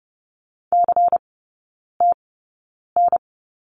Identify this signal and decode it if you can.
Morse code, heard as a keyed tone, about 30 wpm, spaced as for 9 wpm, 690 Hz. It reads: CTN
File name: Lesson 16e - Letter c - Three Characters 30wpm